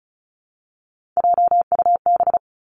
Morse code, heard as a keyed tone, about 36 wpm, 700 Hz. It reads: JUB
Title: Three Challenging Letter-Number Combinations 35wpm